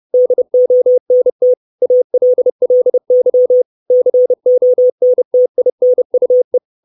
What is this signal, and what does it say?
Morse code, heard as a keyed tone, about 30 wpm, 500 Hz. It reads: DONT ALLY CONTINUE
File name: 632 Words 30WPM